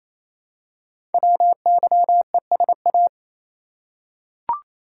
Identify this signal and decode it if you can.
Morse code, heard as a keyed tone, about 27 wpm, 700 Hz. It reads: WYESA
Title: Code Groups - 5 Letters - 28wpm